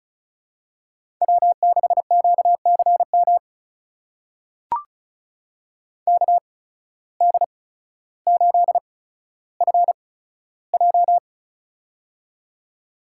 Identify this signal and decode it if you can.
Morse code, heard as a keyed tone, about 36 wpm, spaced as for 15 wpm, 700 Hz. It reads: W6QCM KD8FJ
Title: USA Call Signs - 5 Characters - 8x Character Spacing - 35wpm